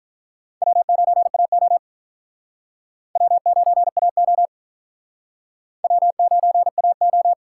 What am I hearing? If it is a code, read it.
Morse code, heard as a keyed tone, about 44 wpm, 700 Hz. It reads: W9AO W9AO W9AO
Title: Speed Racing - USA 1x2 Call Signs 35wpm